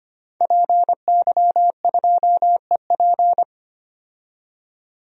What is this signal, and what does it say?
Morse code, heard as a keyed tone, about 25 wpm, 700 Hz. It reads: PY2EP
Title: DXCC-100 Call Signs - 5 Characters - 25wpm